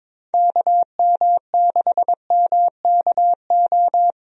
Morse code, heard as a keyed tone, about 22 wpm, 700 Hz. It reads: KM6MKO
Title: USA Call Signs - 6 Characters - Repeated 3x - 22wpm